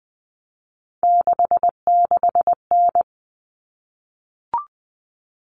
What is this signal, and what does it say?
Morse code, heard as a keyed tone, about 20 wpm, 700 Hz. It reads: 66N
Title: All Cut Numbers - 3-Digit 20wpm